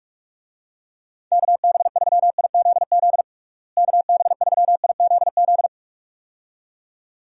Morse code, heard as a keyed tone, about 46 wpm, 700 Hz. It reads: KB3IZZ KB3IZZ
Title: USA Call Signs - 6 Characters - Repeated 2x - 45wpm